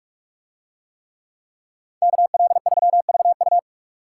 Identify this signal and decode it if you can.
Morse code, heard as a keyed tone, about 46 wpm, 700 Hz. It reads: KL3VU